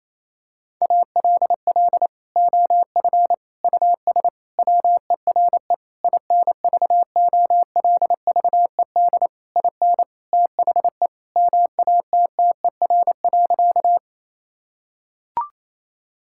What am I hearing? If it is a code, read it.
Morse code, heard as a keyed tone, about 28 wpm, 700 Hz. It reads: ALL OF US WERE INVOLVED IN THE MATTER.
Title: Sentences from Top 1000 Words - Review Part 5 - 28wpm